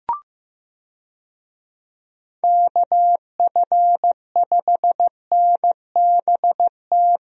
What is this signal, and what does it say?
Morse code, heard as a keyed tone, about 15 wpm, 700 Hz. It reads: KF5NBT
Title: USA Call Signs - 6 Characters - Repeated 3x - 15wpm